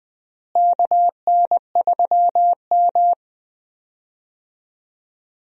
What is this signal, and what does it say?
Morse code, heard as a keyed tone, about 20 wpm, 700 Hz. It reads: KN3M
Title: USA Call Signs - 4 Characters - Repeated 2x - 20wpm